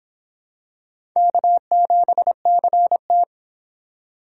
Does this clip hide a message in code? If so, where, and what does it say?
Yes: Morse code from 1.2 to 3.2 s, heard as a keyed tone, about 23 wpm, 700 Hz. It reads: K7CT